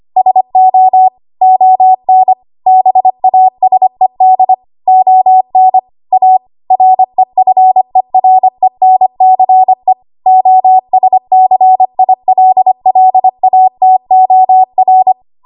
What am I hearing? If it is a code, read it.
Morse code, heard as a keyed tone, about 25 wpm, 750 Hz. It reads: SO ON BASED ON A REFERENCE OSCILLATOR
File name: Archive / 25 WPM / 25WPM